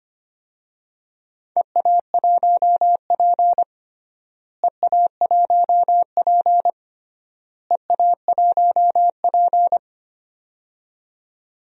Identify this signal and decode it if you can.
Morse code, heard as a keyed tone, about 25 wpm, 700 Hz. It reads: EA1P EA1P EA1P
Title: DXCC-100 Call Signs - 4 Characters - Repeated 3x - 25wpm